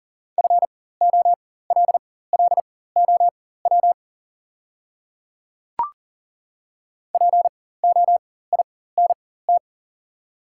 Morse code, heard as a keyed tone, about 40 wpm, spaced as for 19 wpm, 700 Hz. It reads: FOLLOW POINT
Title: 4x Character Spacing - Top 300 Words - 40wpm